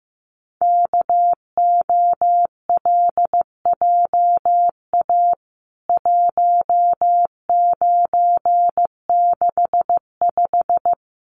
Morse code, heard as a keyed tone, about 15 wpm, 700 Hz. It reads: KOLJA 1965
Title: CWT Exchange 15wpm